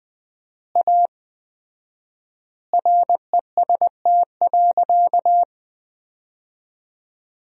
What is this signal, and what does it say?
Morse code, heard as a keyed tone, about 20 wpm, 700 Hz. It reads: A REST.
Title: 4x Word-Spacing - Sentences from Top 500 Words - 20wpm